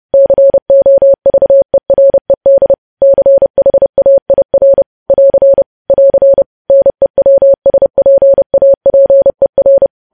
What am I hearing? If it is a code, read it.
Morse code, heard as a keyed tone, about 30 wpm, 550 Hz. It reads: COVERED CHAIR <AR> <AR> NEWSPAPER
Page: News Headlines in Morse Code at 30 WPM